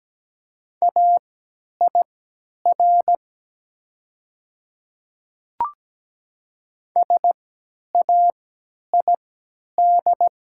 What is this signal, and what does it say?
Morse code, heard as a keyed tone, about 17 wpm, spaced as for 10 wpm, 700 Hz. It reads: AIR SAID
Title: 3x Character Spacing - Top 200 Words - 17wpm